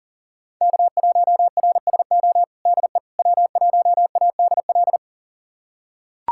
Morse code, heard as a keyed tone, about 37 wpm, 700 Hz. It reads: K1RSO DE W1ADL
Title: USA Callsign DE Callsign - 5 Characters - 40wpm